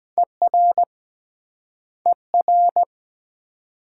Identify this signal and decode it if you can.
Morse code, heard as a keyed tone, about 23 wpm, 700 Hz. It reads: ER ER